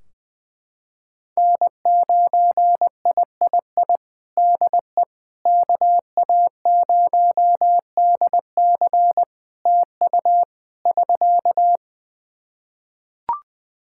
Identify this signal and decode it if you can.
Morse code, heard as a keyed tone, about 20 wpm, 700 Hz. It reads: N9III DE KA0DC TU <SK>
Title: Practice QSO - Exchange 4 - Ending - 20wpm